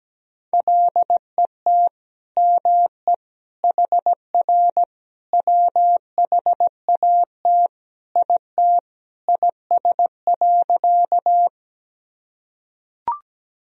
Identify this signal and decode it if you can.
Morse code, heard as a keyed tone, about 17 wpm, 700 Hz. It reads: LET ME HR WHAT IT IS.
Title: Sentences from Top 300 Words and Abbreviations 17wpm